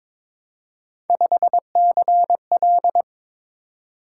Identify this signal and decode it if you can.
Morse code, heard as a keyed tone, about 22 wpm, 700 Hz. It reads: 5CL
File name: practice / 22wpm / Three Letter-Number Combinations - 22wpm